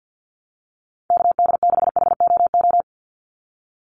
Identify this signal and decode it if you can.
Morse code, heard as a keyed tone, about 49 wpm, 700 Hz. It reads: KD6HOO
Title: USA Call Signs 50wpm